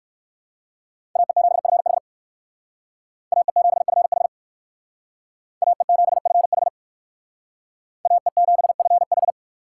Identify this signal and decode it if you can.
Morse code, heard as a keyed tone, about 59 wpm, 700 Hz. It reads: AE7FH AE7FH AE7FH AE7FH